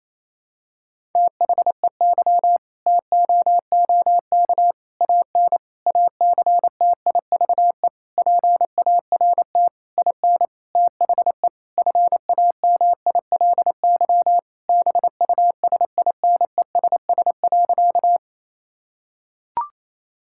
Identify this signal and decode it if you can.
Morse code, heard as a keyed tone, about 28 wpm, 700 Hz. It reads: THEY TOOK AN ACTIVE PART IN THE FAMILY BUSINESS.
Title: Sentences from Top 1000 Words - Review Part 6 - 28wpm